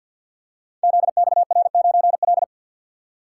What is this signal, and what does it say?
Morse code, heard as a keyed tone, about 50 wpm, 700 Hz. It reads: ZXR9L
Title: Code Groups - 5 Letter:Numbers - 50wpm